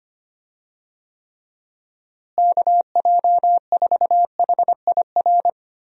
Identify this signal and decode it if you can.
Morse code, heard as a keyed tone, about 25 wpm, 700 Hz. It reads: KJ4HIR
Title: USA Call Signs - 6 Characters - Repeated 3x - 25wpm